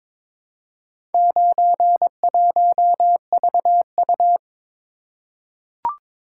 Morse code, heard as a keyed tone, about 22 wpm, 700 Hz. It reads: 91VU
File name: practice / 22wpm / Four Letter-Number Combinations - 22wpm